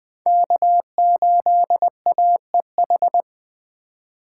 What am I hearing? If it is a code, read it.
Morse code, heard as a keyed tone, about 20 wpm, 700 Hz. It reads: K8AEH
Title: USA Call Signs - 5 Characters - 8x Character Spacing - 20wpm